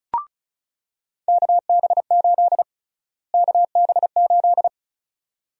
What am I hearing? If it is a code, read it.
Morse code, heard as a keyed tone, about 34 wpm, 700 Hz. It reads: KB8 KB8
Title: USA Call Sign Prefixes - Repeated 2x - 35wpm